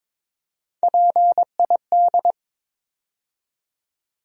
Morse code, heard as a keyed tone, about 22 wpm, 700 Hz. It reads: PID
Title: USA Call Sign Postfixes - 3 Letters 22wpm